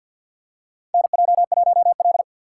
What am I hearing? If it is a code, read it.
Morse code, heard as a keyed tone, about 53 wpm, 700 Hz. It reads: NJ1L
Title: USA Call Signs - 4 Characters - 2x Character Spacing - 50wpm